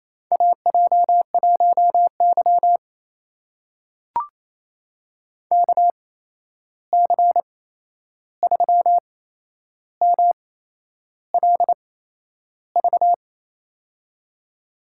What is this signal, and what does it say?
Morse code, heard as a keyed tone, about 28 wpm, spaced as for 11 wpm, 700 Hz. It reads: AJ1Y KC3MLV
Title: USA Call Signs - 8x Character Spacing - 28wpm